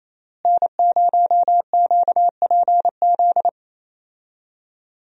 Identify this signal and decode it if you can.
Morse code, heard as a keyed tone, about 28 wpm, 700 Hz. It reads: N0QPZ
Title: USA Call Signs - 5 Characters - Repeated 3x - 28wpm